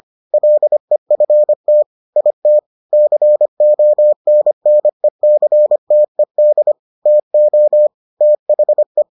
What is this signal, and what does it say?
Morse code, heard as a keyed tone, about 25 wpm, 600 Hz. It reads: LEFT IT CONNECTED TO THE